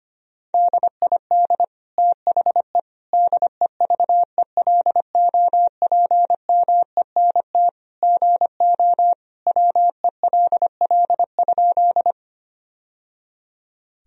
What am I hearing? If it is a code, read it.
Morse code, heard as a keyed tone, about 25 wpm, 700 Hz. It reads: DID THE DEVELOPMENT GO WELL?
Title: Sentences from Top 1000 Words - Review Part 4 - 25wpm